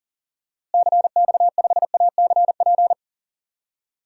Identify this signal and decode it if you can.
Morse code, heard as a keyed tone, about 40 wpm, 700 Hz. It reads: CX5ACP
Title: DXCC-100 Call Signs - 6 Characters - Repeated 2x - 40wpm